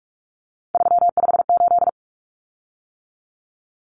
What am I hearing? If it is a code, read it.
Morse code, heard as a keyed tone, about 44 wpm, 700 Hz. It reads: 358
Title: Common Cut Numbers - 3-Digit 45wpm